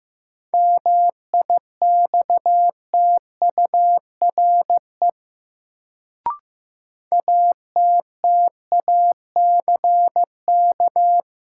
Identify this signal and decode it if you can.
Morse code, heard as a keyed tone, about 15 wpm, 700 Hz. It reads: MIXTURE ATTACK
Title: Sentences from Top 1500 Words - New Word Drill - 15wpm